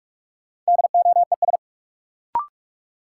Morse code, heard as a keyed tone, about 46 wpm, 700 Hz. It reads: DOES
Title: Sets of 2 Words - New Word Drill - 45wpm